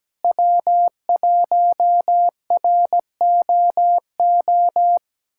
Morse code, heard as a keyed tone, about 17 wpm, 700 Hz. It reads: W1ROO
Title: USA Callsign DE Callsign - 5 Characters - 17wpm